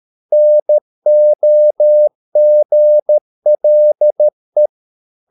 Morse code, heard as a keyed tone, about 13 wpm, 600 Hz. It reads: NOGLE
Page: Ferie 13 wpm